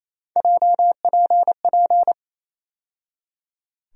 Morse code, heard as a keyed tone, about 28 wpm, 700 Hz. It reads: JPP
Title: USA Call Sign Postfixes - 3 Letters 28wpm